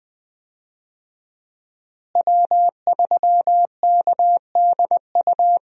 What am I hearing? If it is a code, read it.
Morse code, heard as a keyed tone, about 20 wpm, 700 Hz. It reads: W3KDU
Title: USA Call Signs - 5 Characters - Repeated 3x - 20wpm